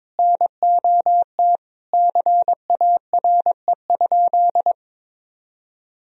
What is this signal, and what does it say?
Morse code, heard as a keyed tone, about 21 wpm, 700 Hz. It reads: NOT CARE?
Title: Sentences from Top 1000 Words - Review Part 1 - 22wpm